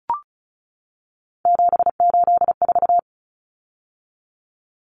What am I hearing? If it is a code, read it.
Morse code, heard as a keyed tone, about 34 wpm, 700 Hz. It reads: 784
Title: Numbers - 3-Digit 35wpm